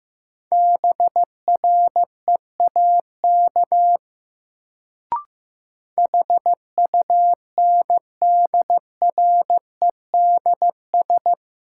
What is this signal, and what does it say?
Morse code, heard as a keyed tone, about 15 wpm, 700 Hz. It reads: BREAK HUNDREDS
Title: Sets of 4 Words - New Word Drill - 15wpm